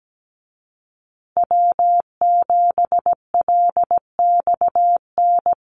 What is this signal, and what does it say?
Morse code, heard as a keyed tone, about 17 wpm, 700 Hz. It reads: W7LXN
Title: Call Signs Repeated 3x - 17wpm